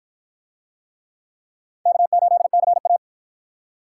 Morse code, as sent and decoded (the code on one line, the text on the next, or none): -.- ---.. -.-. .-
K8CA